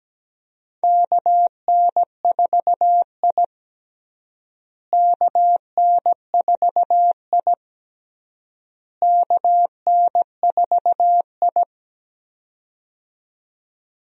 Morse code, heard as a keyed tone, about 17 wpm, 700 Hz. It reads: KN4I KN4I KN4I
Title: USA Call Signs - 4 Characters - Repeated 3x - 17wpm